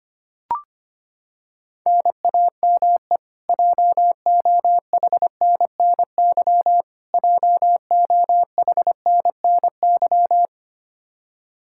Morse code, heard as a keyed tone, about 25 wpm, 700 Hz. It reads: NAME JOHNNY JOHNNY
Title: Practice QSO - Exchange 1 - Beginning - 25wpm